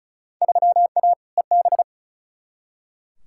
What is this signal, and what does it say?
Morse code, heard as a keyed tone, about 36 wpm, 700 Hz. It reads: 3A EB
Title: Field Day Exchanges Encore - 35wpm